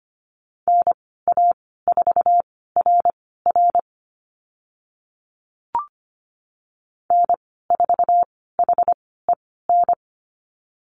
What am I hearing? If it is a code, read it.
Morse code, heard as a keyed tone, about 25 wpm, spaced as for 16 wpm, 690 Hz. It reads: NA4RR N4HEN
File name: Lesson 15g - Number 5 - Call Signs 25wpm